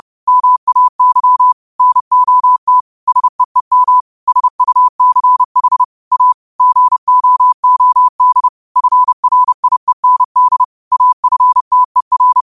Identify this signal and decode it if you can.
Morse code, heard as a keyed tone, about 30 wpm, 1000 Hz. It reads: MAY NOT SEEM SUCH A GOOD FRIEND AFTER
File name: qotdfiles / 30wpm